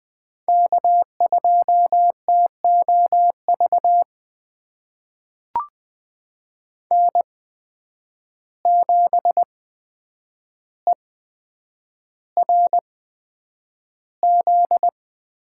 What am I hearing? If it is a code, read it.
Morse code, heard as a keyed tone, about 20 wpm, spaced as for 9 wpm, 700 Hz. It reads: K2TOV N7ERZ
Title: USA Call Signs - 5 Characters - 8x Character Spacing - 20wpm